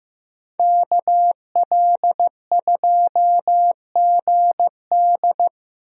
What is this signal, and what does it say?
Morse code, heard as a keyed tone, about 15 wpm, 700 Hz. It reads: KL2GD
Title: USA Call Signs - 5 Characters - Repeated 2x - 15wpm